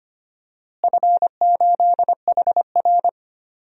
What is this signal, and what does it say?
Morse code, heard as a keyed tone, about 25 wpm, 700 Hz. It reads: F8HR
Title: DXCC-100 Call Signs - 4 Characters - Repeated 3x - 25wpm